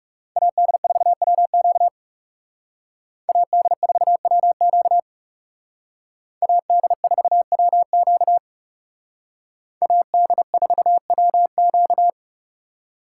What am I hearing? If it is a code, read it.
Morse code, heard as a keyed tone, about 38 wpm, 700 Hz. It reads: AD4WQ AD4WQ AD4WQ AD4WQ